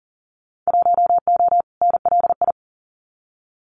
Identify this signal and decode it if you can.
Morse code, heard as a keyed tone, about 38 wpm, 700 Hz. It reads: 1O NLI
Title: Winter Field Day Exchanges 40wpm